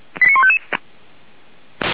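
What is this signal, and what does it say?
Five-tone selective call (ZVEI), heard as quick runs of five tones, tones of about 80 ms. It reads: 89155
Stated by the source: Five-tone ZVEI selective call, received by SDR.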